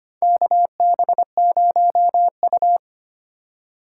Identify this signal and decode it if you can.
Morse code, heard as a keyed tone, about 25 wpm, 700 Hz. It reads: KB0U
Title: USA Call Signs - 4 Characters - 6x Character Spacing - 25wpm